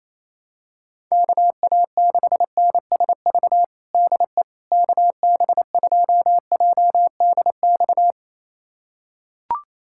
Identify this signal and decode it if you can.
Morse code, heard as a keyed tone, about 28 wpm, 700 Hz. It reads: KA6NSV DE KB2JDX
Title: USA Callsign DE Callsign 28wpm